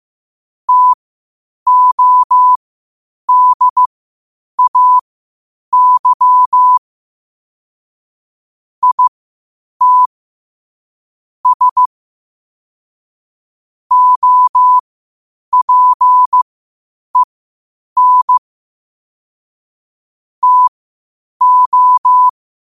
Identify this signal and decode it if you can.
Morse code, heard as a keyed tone, about 15 wpm, spaced as for 8 wpm, 1000 Hz. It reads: TODAY IT S OPEN TO
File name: qotdfiles / 8wpm